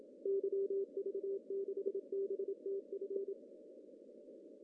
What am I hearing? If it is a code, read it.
Morse code, heard as a keyed tone, about 27 wpm, 390 Hz. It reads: YV6BTF
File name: CW / cqww-cw-k5tr